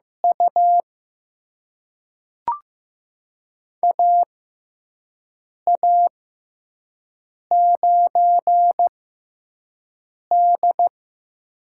Morse code, heard as a keyed tone, about 15 wpm, spaced as for 5 wpm, 700 Hz. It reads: U AA9D
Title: USA Call Signs - 6x Character Spacing - 15wpm